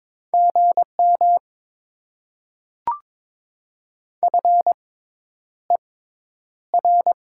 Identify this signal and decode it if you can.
Morse code, heard as a keyed tone, about 22 wpm, spaced as for 10 wpm, 690 Hz. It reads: GM FER
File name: Top 20 QSO Elements - 6x Character Spacing - 22wpm